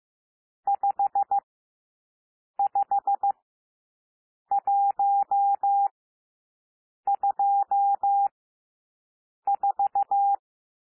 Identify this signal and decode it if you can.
Morse code, heard as a keyed tone, about 15 wpm, spaced as for 6 wpm, 800 Hz. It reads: 55124
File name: SOUND NUMBERS 15 wpm